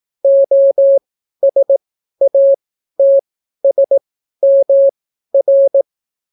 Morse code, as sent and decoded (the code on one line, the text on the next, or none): --- ... .- - ... -- .-.
OSATSMR